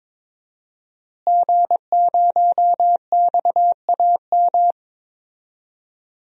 Morse code, heard as a keyed tone, about 22 wpm, 700 Hz. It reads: G0XAM
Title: DXCC-100 Call Signs - 5 Characters - Repeated 3x - 22wpm